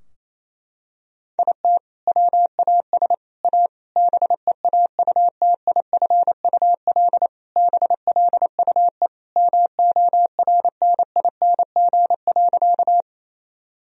Morse code, heard as a keyed tone, about 28 wpm, 700 Hz. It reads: IT WAS A BEAUTIFUL BLUE MORNING.
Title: Sentences from Top 1000 Words - Review Part 3 - 28wpm